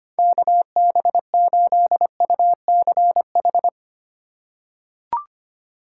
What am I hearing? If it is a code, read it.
Morse code, heard as a keyed tone, about 25 wpm, 700 Hz. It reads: KB8UCH
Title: USA Call Signs - 6 Characters - 3x Character Spacing - 25wpm